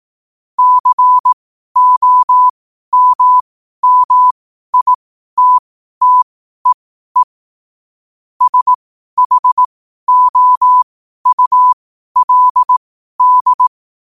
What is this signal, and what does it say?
Morse code, heard as a keyed tone, about 18 wpm, spaced as for 12 wpm, 1000 Hz. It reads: COMMITTEE SHOULD
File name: qotdfiles / 12wpm